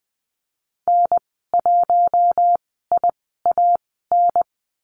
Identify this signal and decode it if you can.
Morse code, heard as a keyed tone, about 20 wpm, spaced as for 15 wpm, 700 Hz. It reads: N1IAN
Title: 076 Lesson 13g - Letter L - Call Signs 20wpm